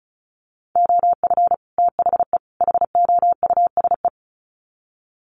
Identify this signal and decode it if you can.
Morse code, heard as a keyed tone, about 34 wpm, 700 Hz. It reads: OF THE HOUSE
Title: Sets of 3 Words 35wpm